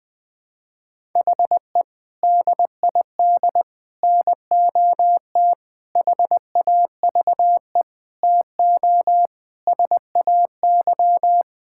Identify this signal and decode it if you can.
Morse code, heard as a keyed tone, about 20 wpm, 700 Hz. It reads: HE DID NOT HAVE TO SAY
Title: Sentences with 5-Word Phrases - 20wpm